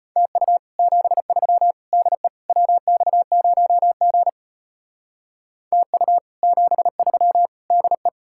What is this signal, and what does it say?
Morse code, heard as a keyed tone, about 29 wpm, 700 Hz. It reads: TU 73 DE WX0G TU 73 DE
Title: Speed-Racing Common QSO Phrases - 25wpm